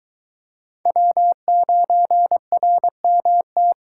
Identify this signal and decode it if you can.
Morse code, heard as a keyed tone, about 22 wpm, 700 Hz. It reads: W9RMT